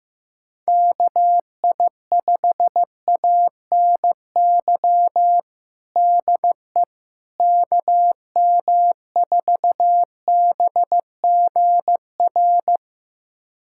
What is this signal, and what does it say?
Morse code, heard as a keyed tone, about 15 wpm, 700 Hz. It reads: KI5ANY DE KM4BGR
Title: USA Callsign DE Callsign 15wpm